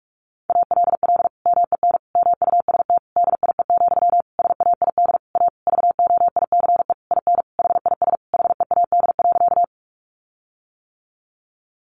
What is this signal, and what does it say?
Morse code, heard as a keyed tone, about 46 wpm, 700 Hz. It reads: ALL MEN MUST DIE, SAID A VOICE IN HIS HEAD.
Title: Sentences from Top 1300 Words 45wpm